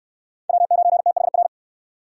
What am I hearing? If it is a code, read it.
Morse code, heard as a keyed tone, about 68 wpm, 700 Hz. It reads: W9THR